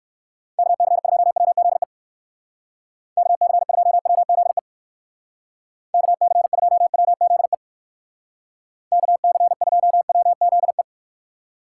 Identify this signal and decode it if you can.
Morse code, heard as a keyed tone, about 57 wpm, 700 Hz. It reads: KC2WZE KC2WZE KC2WZE KC2WZE